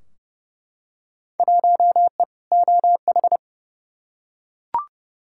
Morse code, heard as a keyed tone, about 30 wpm, 700 Hz. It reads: 1E OH
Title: Field Day Exchanges - 30wpm